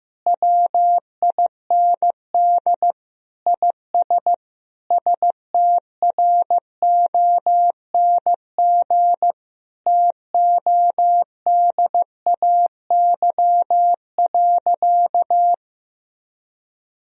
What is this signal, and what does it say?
Morse code, heard as a keyed tone, about 15 wpm, 700 Hz. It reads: WIND IS STRONG TODAY.
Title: Sentences from Top 1000 Words - Review Part 2 - 15wpm